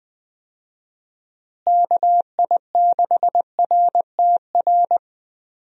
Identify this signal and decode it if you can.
Morse code, heard as a keyed tone, about 20 wpm, 700 Hz. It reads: KI6RTR
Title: USA Call Signs - 6 Characters - Repeated 2x - 20wpm